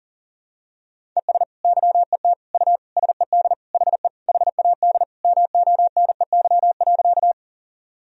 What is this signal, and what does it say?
Morse code, heard as a keyed tone, about 40 wpm, 700 Hz. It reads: ES YET U SED HE HAD MONEY.
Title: Sentences from Top 500 Words and Abbreviations 40wpm